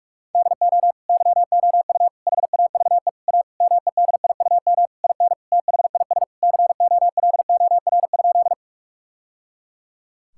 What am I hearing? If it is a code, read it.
Morse code, heard as a keyed tone, about 41 wpm, 690 Hz. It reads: DO YOU HAVE A MEDIUM IN THIS COLOR?
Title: Sentences from Top 1000 Words - Review Part 6 - 45wpm